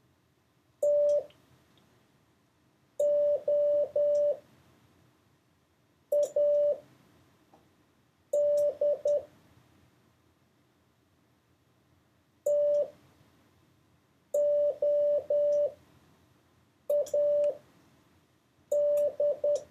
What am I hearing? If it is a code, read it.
Morse code, heard as a keyed tone, about 10 wpm, spaced as for 4 wpm, 580 Hz. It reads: TOAD TOAD
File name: 4of6-words-1